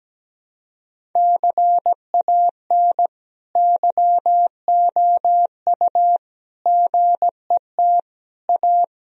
Morse code, heard as a keyed tone, about 17 wpm, 700 Hz. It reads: CAN YOU GET A
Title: Sentences from Top 1000 Words - Review Part 1 - 17wpm